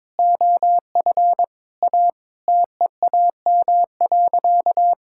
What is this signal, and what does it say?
Morse code, heard as a keyed tone, about 22 wpm, 700 Hz. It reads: OF A TEAM.
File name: practice / 22wpm / Sentences with 4-Word Phrases - 22wpm